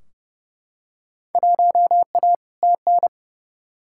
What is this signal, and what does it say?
Morse code, heard as a keyed tone, about 31 wpm, 700 Hz. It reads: 1A TN
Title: Field Day Exchanges - 30wpm